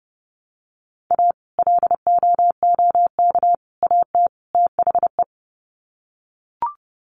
Morse code, heard as a keyed tone, about 30 wpm, 700 Hz. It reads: A LOOK AT THE
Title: Sets of 4 Words 30wpm